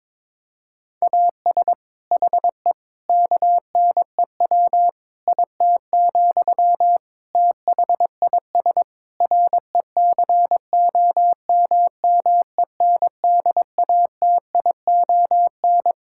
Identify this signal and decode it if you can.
Morse code, heard as a keyed tone, about 22 wpm, 700 Hz. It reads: AS HE KNEW IT, THIS RECOMMENDATION
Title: Sentences from Top 1600 Words 22wpm